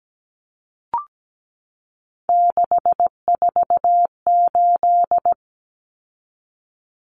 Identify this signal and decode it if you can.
Morse code, heard as a keyed tone, about 17 wpm, 700 Hz. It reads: 648
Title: Numbers - 3-Digit 17wpm